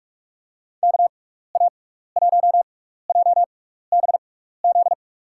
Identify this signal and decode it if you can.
Morse code, heard as a keyed tone, about 44 wpm, spaced as for 15 wpm, 700 Hz. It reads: KA1JBZ
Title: USA Call Signs - 6 Characters - 6x Character Spacing - 45wpm